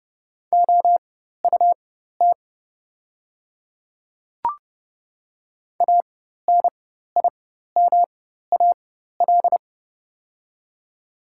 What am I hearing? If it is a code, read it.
Morse code, heard as a keyed tone, about 30 wpm, spaced as for 14 wpm, 700 Hz. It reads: OUT ANIMAL
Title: 4x Character Spacing - Top 200 Words - 30wpm